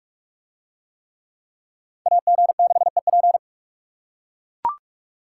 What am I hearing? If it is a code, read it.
Morse code, heard as a keyed tone, about 42 wpm, 700 Hz. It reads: AG6EP